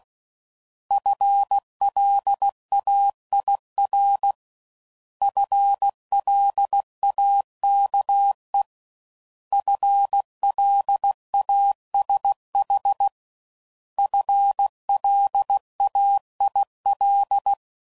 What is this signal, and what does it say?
Morse code, heard as a keyed tone, about 16 wpm, 800 Hz. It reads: FLAIR FLAKE FLASH FLAIL